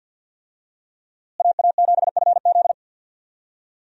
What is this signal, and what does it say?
Morse code, heard as a keyed tone, about 48 wpm, 700 Hz. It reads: AA7FZ
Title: USA Call Signs - 5 Characters - Repeated 2x - 50wpm